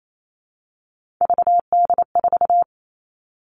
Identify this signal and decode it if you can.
Morse code, heard as a keyed tone, about 28 wpm, 700 Hz. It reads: VD4
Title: All Cut Numbers - 3-Digit 28wpm